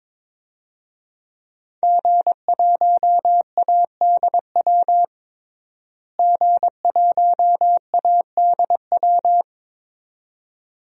Morse code, heard as a keyed tone, about 22 wpm, 700 Hz. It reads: G1ADW G1ADW
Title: DXCC-100 Call Signs - 5 Characters - Repeated 2x - 22wpm